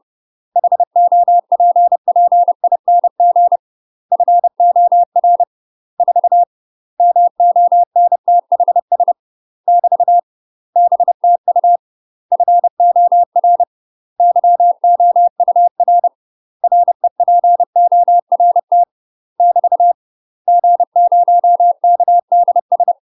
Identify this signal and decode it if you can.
Morse code, heard as a keyed tone, about 30 wpm, 700 Hz. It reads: HOPPING FOR 4 MONTHS <BT> BTU FOR YOUR REPORT <BT> G0KDS